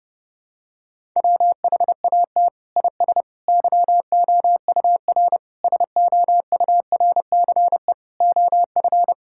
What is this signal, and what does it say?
Morse code, heard as a keyed tone, about 32 wpm, 700 Hz. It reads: WHAT IS YOUR SOURCE OF
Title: Sentences from Top 1000 Words - Review Part 5 - 30wpm